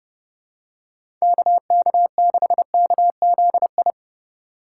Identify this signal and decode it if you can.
Morse code, heard as a keyed tone, about 30 wpm, 700 Hz. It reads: KK6KZI
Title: USA Call Signs - 6 Characters - 2x Character Spacing - 30wpm